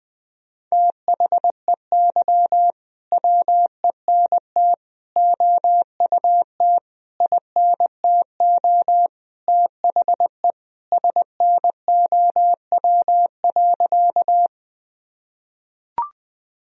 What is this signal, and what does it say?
Morse code, heard as a keyed tone, about 20 wpm, 700 Hz. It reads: THEY WENT OUT INTO THE SNOW.
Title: Sentences from Top 1000 Words - Review Part 3 - 20wpm